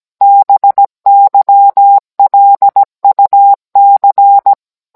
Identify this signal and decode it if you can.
Morse code, heard as a keyed tone, about 17 wpm, 800 Hz. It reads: BYLUC